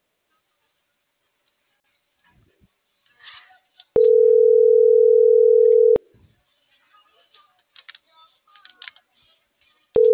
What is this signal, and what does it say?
Ringback tone (North America), 440 + 480 Hz.